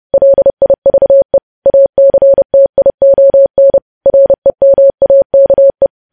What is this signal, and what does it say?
Morse code, heard as a keyed tone, about 30 wpm, 550 Hz. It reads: LIVE ACTION REMAKE
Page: news headlines in morse code at 30 wpm